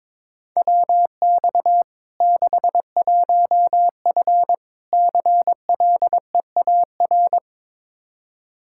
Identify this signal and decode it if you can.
Morse code, heard as a keyed tone, about 22 wpm, 700 Hz. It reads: WX 61F CLEAR
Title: K1USN SST QSO Practice 22wpm